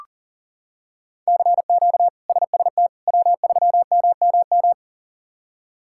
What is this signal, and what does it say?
Morse code, heard as a keyed tone, about 40 wpm, 700 Hz. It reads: CQ SST W3MMM
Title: K1USN SST QSO Practice - 40wpm